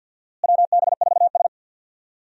Morse code, heard as a keyed tone, about 52 wpm, 700 Hz. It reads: WB4S
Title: USA Call Signs - 4 Characters - Repeated 2x - 50wpm